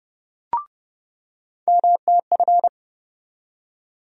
Morse code, heard as a keyed tone, about 30 wpm, 700 Hz. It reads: MTF
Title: USA Call Sign Postfixes - 3 Letters 30wpm